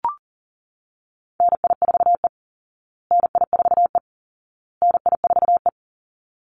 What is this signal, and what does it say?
Morse code, heard as a keyed tone, about 40 wpm, 700 Hz. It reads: NI4E NI4E NI4E
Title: Call Signs Repeated 2x - 40wpm